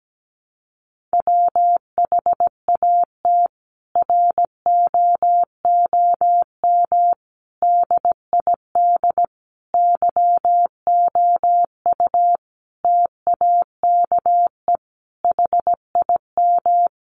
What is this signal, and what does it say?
Morse code, heard as a keyed tone, about 17 wpm, 700 Hz. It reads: WHAT ROOM DID YOU TAKE HIM
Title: Sentences from Top 600 Words 17wpm